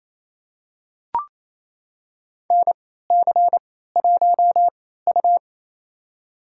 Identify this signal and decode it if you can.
Morse code, heard as a keyed tone, about 28 wpm, spaced as for 16 wpm, 700 Hz. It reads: NC1U
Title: USA Call Signs - 4 Characters - 3x Character Spacing - 28wpm